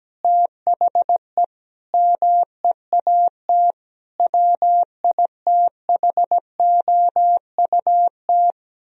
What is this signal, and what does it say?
Morse code, heard as a keyed tone, about 17 wpm, 700 Hz. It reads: THE MEAT WITHOUT
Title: Sentences from Top 1000 Words - Review Part 3 - 17wpm